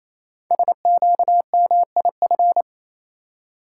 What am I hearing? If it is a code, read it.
Morse code, heard as a keyed tone, about 28 wpm, 700 Hz. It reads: SQMIF
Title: Code Groups - 5 Letters - 28wpm